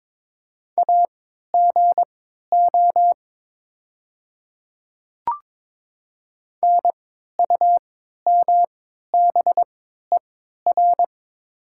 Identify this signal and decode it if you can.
Morse code, heard as a keyed tone, about 22 wpm, spaced as for 12 wpm, 690 Hz. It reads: AGO NUMBER
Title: 3x Character Spacing - Top 500 Words - 22wpm